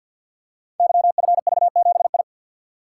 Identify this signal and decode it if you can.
Morse code, heard as a keyed tone, about 48 wpm, 700 Hz. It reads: YFV7I
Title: Code Groups - 5 Letter:Numbers - 50wpm